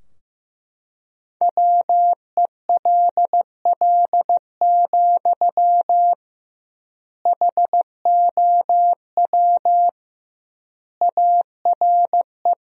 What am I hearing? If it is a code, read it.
Morse code, heard as a keyed tone, about 16 wpm, 700 Hz. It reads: WELL, HOW ARE
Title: 2x Word-Spacing - Sentences from Top 200 Words - 15wpm